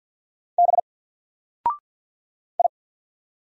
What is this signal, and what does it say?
Morse code, heard as a keyed tone, about 48 wpm, 700 Hz. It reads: B I
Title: Single Letter-Number 50wpm